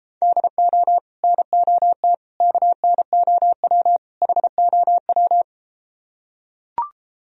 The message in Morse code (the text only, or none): DO NOT KNOW HOW